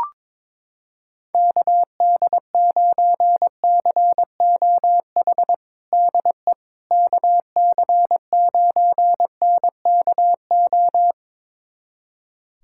Morse code, heard as a keyed tone, about 22 wpm, 700 Hz. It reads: KD9COH DE KC9NKO
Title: USA Callsign DE Callsign - 6 Characters - 22wpm